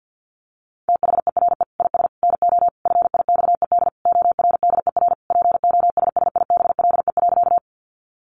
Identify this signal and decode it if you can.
Morse code, heard as a keyed tone, about 50 wpm, 700 Hz. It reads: THERE IS NO FIXED ORDER POSSIBLE.
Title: Sentences from Top 1100 Words 50wpm